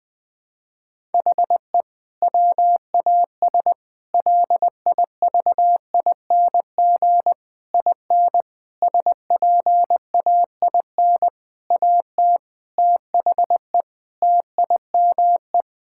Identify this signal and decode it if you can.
Morse code, heard as a keyed tone, about 20 wpm, 700 Hz. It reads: HE WAS LIVING IN SPAIN AT THE TIME
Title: Sentences with 6-Word Phrases - Spoken Beginning - 20wpm